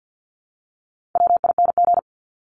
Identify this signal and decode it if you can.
Morse code, heard as a keyed tone, about 50 wpm, 700 Hz. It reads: WING
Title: Top 1001-1250 Words 50wpm